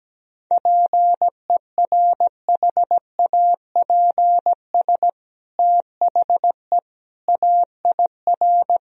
Morse code, heard as a keyed tone, about 17 wpm, 700 Hz. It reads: PERHAPS THE AIR
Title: Sentences from Top 1000 Words - Review Part 1 - 17wpm